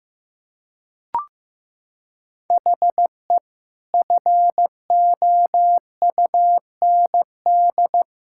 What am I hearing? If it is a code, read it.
Morse code, heard as a keyed tone, about 16 wpm, 700 Hz. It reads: HE FOUND
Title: Sentences from Top 1000 Words - Review Part 6 - 15wpm